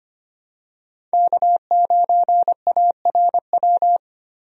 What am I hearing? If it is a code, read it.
Morse code, heard as a keyed tone, about 25 wpm, 700 Hz. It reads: K9ARW
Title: USA Call Signs - 5 Characters - 25wpm